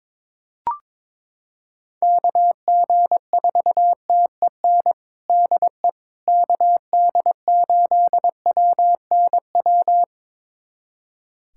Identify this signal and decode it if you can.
Morse code, heard as a keyed tone, about 22 wpm, 700 Hz. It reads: KG4TEN DE KD8WNW
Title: USA Callsign DE Callsign - 6 Characters - 22wpm